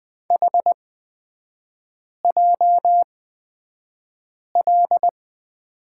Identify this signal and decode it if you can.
Morse code, heard as a keyed tone, about 20 wpm, spaced as for 5 wpm, 700 Hz. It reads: HJL
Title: 255 Lesson 41e - Period - Three Characters 20wpm